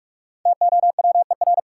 Morse code, heard as a keyed tone, about 46 wpm, 700 Hz. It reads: TOWER
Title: Five Letter Words - 45wpm